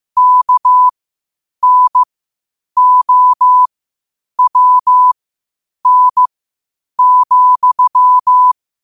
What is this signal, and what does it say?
Morse code, heard as a keyed tone, about 15 wpm, spaced as for 8 wpm, 1000 Hz. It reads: KNOWN,
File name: qotdfiles / 8wpm